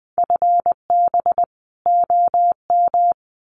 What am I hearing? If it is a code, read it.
Morse code, heard as a keyed tone, about 20 wpm, 700 Hz. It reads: FB OM
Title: Common QSO Phrases 20wpm